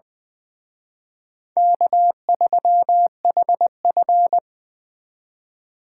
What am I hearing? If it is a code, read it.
Morse code, heard as a keyed tone, about 24 wpm, 700 Hz. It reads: K3HF
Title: Speed Racing - USA Call Signs 20wpm